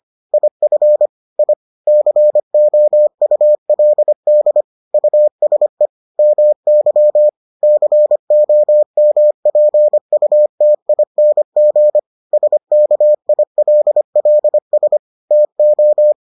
Morse code, heard as a keyed tone, about 25 wpm, 600 Hz. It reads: IF I COULD USE MY COMPUTING SKILLS TO